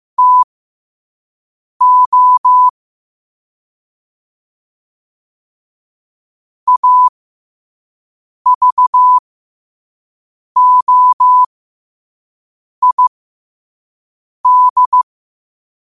Morse code, heard as a keyed tone, about 15 wpm, spaced as for 5 wpm, 1000 Hz. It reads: TO AVOID
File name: qotdfiles / 5wpm